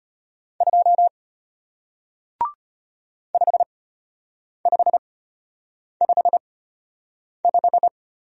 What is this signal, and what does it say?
Morse code, heard as a keyed tone, about 34 wpm, spaced as for 8 wpm, 700 Hz. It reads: 2 5555